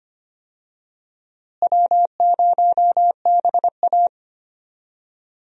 Morse code, heard as a keyed tone, about 25 wpm, 700 Hz. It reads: W0BA
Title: USA Call Signs - 4 Characters - Repeated 2x - 25wpm